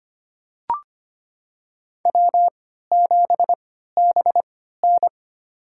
Morse code, heard as a keyed tone, about 25 wpm, spaced as for 14 wpm, 700 Hz. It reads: W7BN
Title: USA Call Signs - 4 Characters - 3x Character Spacing - 25wpm